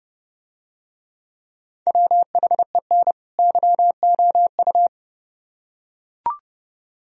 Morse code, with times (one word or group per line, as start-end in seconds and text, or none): 1.9-3.1 WHEN
3.4-4.9 YOU